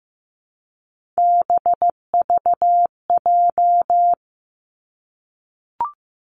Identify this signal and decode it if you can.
Morse code, heard as a keyed tone, about 15 wpm, 700 Hz. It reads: BVJ
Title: Three Challenging Letter-Number Combinations 15wpm